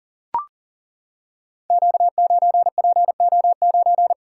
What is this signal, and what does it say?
Morse code, heard as a keyed tone, about 38 wpm, 700 Hz. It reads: Q9PO9
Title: Code Groups - 5 Letter:Numbers - 40wpm